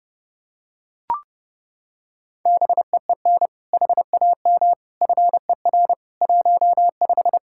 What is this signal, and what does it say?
Morse code, heard as a keyed tone, about 30 wpm, 700 Hz. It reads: BEEN HAM FER 15
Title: K1USN SST QSO Practice 30wpm